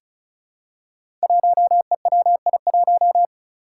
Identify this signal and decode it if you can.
Morse code, heard as a keyed tone, about 36 wpm, 700 Hz. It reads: 1EWI1
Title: Five Challenging Letter-Number Combinations 35wpm